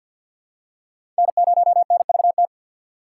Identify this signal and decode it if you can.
Morse code, heard as a keyed tone, about 51 wpm, 700 Hz. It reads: N0NVT
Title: USA Call Signs - 5 Characters - 6x Character Spacing - 50wpm